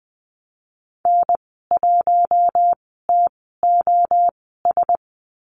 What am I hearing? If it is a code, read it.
Morse code, heard as a keyed tone, about 20 wpm, spaced as for 14 wpm, 700 Hz. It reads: N1TOS
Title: 062 Lesson 11g - Letter h - Call Signs 20wpm